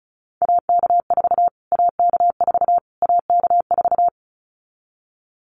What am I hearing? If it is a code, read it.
Morse code, heard as a keyed tone, about 34 wpm, 700 Hz. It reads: AK4 AK4 AK4
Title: Callsign Prefixes for Top 25 DXCC - Repeated 3x - 35wpm